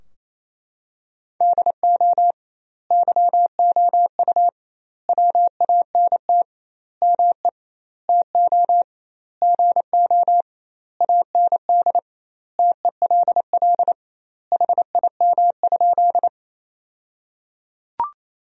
Morse code, heard as a keyed tone, about 27 wpm, 700 Hz. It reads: DO YOU WANT ME TO GO AND TELL HIM?
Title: 2x Word-Spacing - Sentences from Top 200 Words - 28wpm